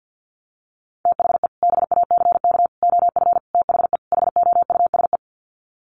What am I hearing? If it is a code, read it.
Morse code, heard as a keyed tone, about 50 wpm, 700 Hz. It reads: THE BACK OF THE HOUSE
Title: Sets of 5 Words Encore2 50wpm